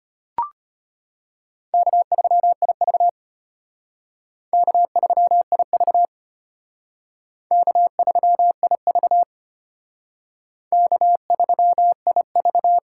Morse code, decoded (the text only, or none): K3IV K3IV K3IV K3IV